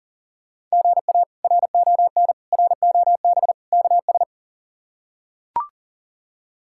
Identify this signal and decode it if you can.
Morse code, heard as a keyed tone, about 40 wpm, 700 Hz. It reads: GA RON ROB KS
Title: K1USN SST QSO Practice - 40wpm